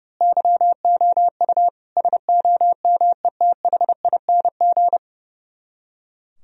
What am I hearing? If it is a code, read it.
Morse code, heard as a keyed tone, about 28 wpm, 700 Hz. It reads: YOU SOMETHING
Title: Speed-Racing - Sets of 5 Words 30wpm